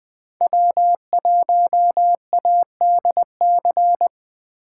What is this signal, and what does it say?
Morse code, heard as a keyed tone, about 22 wpm, 700 Hz. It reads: W1ADC